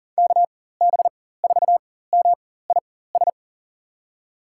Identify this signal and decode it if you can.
Morse code, heard as a keyed tone, about 40 wpm, spaced as for 19 wpm, 700 Hz. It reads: KB4MIS
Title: USA Call Signs - 6 Characters - 4x Character Spacing - 40wpm